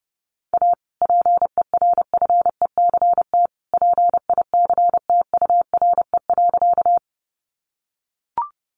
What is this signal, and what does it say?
Morse code, heard as a keyed tone, about 30 wpm, 700 Hz. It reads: A PERFECT PICTURE.
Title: Sentences from Top 1000 Words 30wpm